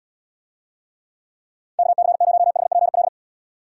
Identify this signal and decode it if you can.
Morse code, heard as a keyed tone, about 71 wpm, 700 Hz. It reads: KC0HPL